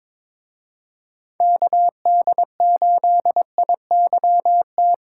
Morse code, heard as a keyed tone, about 27 wpm, 700 Hz. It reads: KD8IYT
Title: Speed Racing - USA Call Signs 22wpm